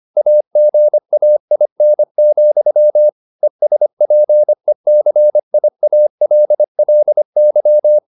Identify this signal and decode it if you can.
Morse code, heard as a keyed tone, about 25 wpm, 600 Hz. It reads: AGAIN, ESPECIALLY